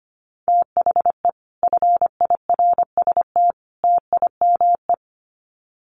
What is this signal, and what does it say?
Morse code, heard as a keyed tone, about 25 wpm, 700 Hz. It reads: THE FIRST TIME
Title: Sets of 3 Words 25wpm